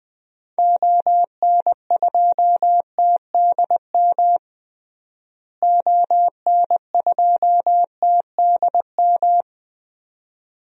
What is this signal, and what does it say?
Morse code, heard as a keyed tone, about 20 wpm, 700 Hz. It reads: ON2TDM ON2TDM
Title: DXCC-100 Call Signs - 6 Characters - Repeated 2x - 20wpm